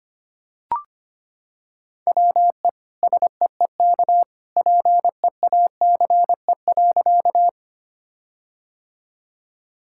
Morse code, heard as a keyed tone, about 25 wpm, 700 Hz. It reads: WE SEEK PEACE.
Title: Sentences from Top 1600 Words 25wpm